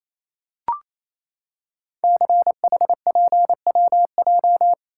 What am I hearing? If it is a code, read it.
Morse code, heard as a keyed tone, about 27 wpm, 700 Hz. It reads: CHPWJ
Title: Code Groups - 5 Letters - 28wpm